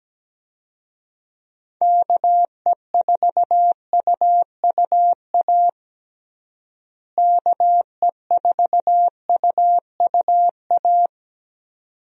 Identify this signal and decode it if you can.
Morse code, heard as a keyed tone, about 17 wpm, 700 Hz. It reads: KE4UUA KE4UUA
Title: USA Call Signs - 6 Characters - Repeated 2x - 17wpm